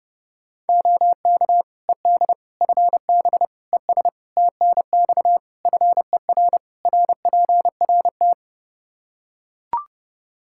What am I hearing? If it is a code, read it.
Morse code, heard as a keyed tone, about 30 wpm, 700 Hz. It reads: OK ED FB ES TNX FER RPRT
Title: K1USN SST QSO Practice 30wpm